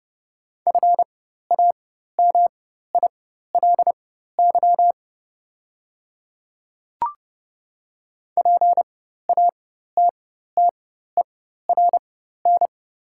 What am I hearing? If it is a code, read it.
Morse code, heard as a keyed tone, about 30 wpm, spaced as for 14 wpm, 690 Hz. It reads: FAMILY PATTERN
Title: 4x Character Spacing - Top 500 Words - 30wpm